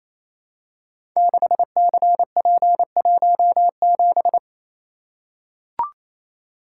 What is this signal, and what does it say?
Morse code, heard as a keyed tone, about 29 wpm, 700 Hz. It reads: 6CP17
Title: Code Groups - 5 Letter:Numbers - 28wpm